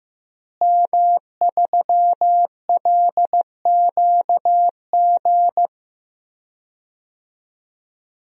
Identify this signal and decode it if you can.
Morse code, heard as a keyed tone, about 15 wpm, 700 Hz. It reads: M3LQG
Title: DXCC-100 Call Signs - 5 Characters - Repeated 2x - 15wpm